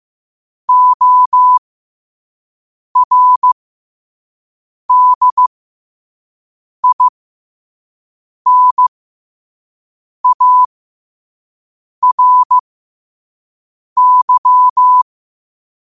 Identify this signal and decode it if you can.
Morse code, heard as a keyed tone, about 15 wpm, spaced as for 5 wpm, 1000 Hz. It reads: ORDINARY
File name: qotdfiles / 5wpm